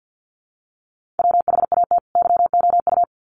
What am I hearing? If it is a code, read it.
Morse code, heard as a keyed tone, about 50 wpm, 700 Hz. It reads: WHAT YOU
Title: Sets of 2 Words 50wpm